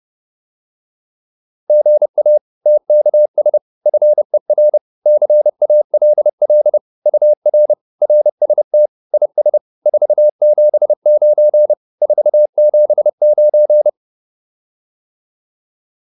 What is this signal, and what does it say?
Morse code, heard as a keyed tone, about 30 wpm, 600 Hz. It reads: GA TKS FER CALL UR RST IS 479 479